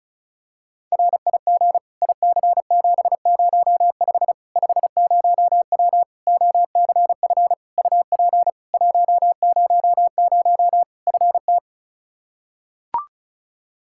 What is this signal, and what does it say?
Morse code, heard as a keyed tone, about 36 wpm, 700 Hz. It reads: RIG IC705 50W OCF UP 100 FT
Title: Practice QSO - Exchange 2 - Info - 35wpm